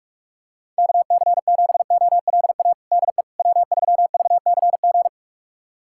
Morse code, heard as a keyed tone, about 46 wpm, 700 Hz. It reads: KC7OLA DE W3VCG
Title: USA Callsign DE Callsign 45wpm